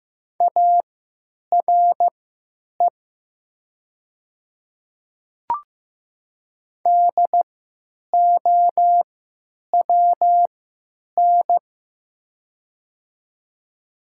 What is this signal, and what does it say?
Morse code, heard as a keyed tone, about 15 wpm, spaced as for 8 wpm, 700 Hz. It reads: ARE DOWN
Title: 3x Character Spacing - Top 100 Words - 15wpm